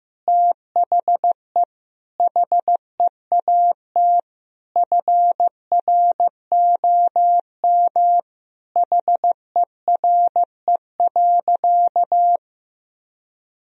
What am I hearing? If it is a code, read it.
Morse code, heard as a keyed tone, about 15 wpm, 700 Hz. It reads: THE HEAT FROM HERE.
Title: Sentences from Top 1000 Words - Review Part 3 - 15wpm